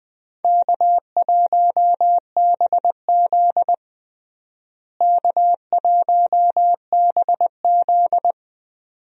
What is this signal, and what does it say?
Morse code, heard as a keyed tone, about 20 wpm, 700 Hz. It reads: K1BZ K1BZ
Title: USA Call Signs - 4 Characters - Repeated 2x - 20wpm